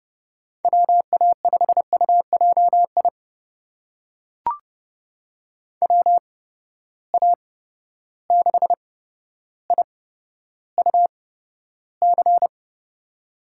Morse code, heard as a keyed tone, about 32 wpm, spaced as for 13 wpm, 700 Hz. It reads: WA5UJI WA6IUC
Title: USA Call Signs - 6 Characters - 8x Character Spacing - 30wpm